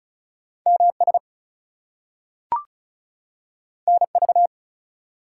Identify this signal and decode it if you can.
Morse code, heard as a keyed tone, about 36 wpm, 700 Hz. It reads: MS NV
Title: US States and Canadian Provinces 35wpm